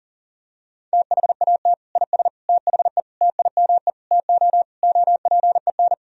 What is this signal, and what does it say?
Morse code, heard as a keyed tone, about 37 wpm, 700 Hz. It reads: THAT IS THE TIME TO OPEN
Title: Sentences from Top 1000 Words - Review Part 1 - 40wpm